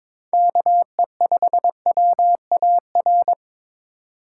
Morse code, heard as a keyed tone, about 25 wpm, 700 Hz. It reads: KE5WAR
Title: Speed Racing - USA 2x3 Call Signs 22wpm